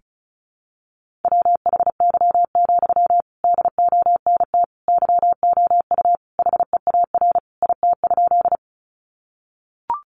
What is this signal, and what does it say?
Morse code, heard as a keyed tone, about 34 wpm, 700 Hz. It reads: WHY, DONT YOU HEAR IT?
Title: Sentences from Top 300 Words 35wpm